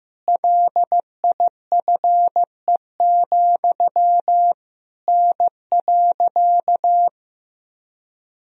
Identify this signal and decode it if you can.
Morse code, heard as a keyed tone, about 16 wpm, 700 Hz. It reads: LIFE, N.
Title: Sentences from Top 300 Words and Abbreviations 15wpm